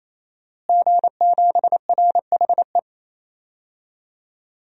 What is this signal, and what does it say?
Morse code, heard as a keyed tone, about 28 wpm, 700 Hz. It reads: G7RHE
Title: DXCC-100 Call Signs - 5 Characters - Repeated 3x - 28wpm